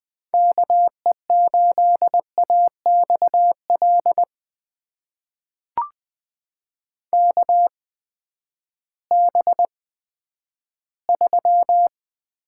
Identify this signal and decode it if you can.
Morse code, heard as a keyed tone, about 20 wpm, spaced as for 11 wpm, 700 Hz. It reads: KE8AXL KB3
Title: USA Call Signs - 8x Character Spacing - 20wpm